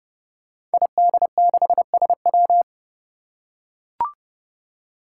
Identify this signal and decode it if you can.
Morse code, heard as a keyed tone, about 30 wpm, 700 Hz. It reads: ID6SW
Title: Five Challenging Letter-Number Combinations 30wpm